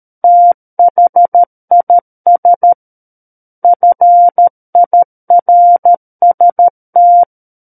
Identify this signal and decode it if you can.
Morse code, heard as a keyed tone, about 13 wpm, 700 Hz. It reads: THIS FIRST
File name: Book One / 04-Chapter 4 - 13 WPM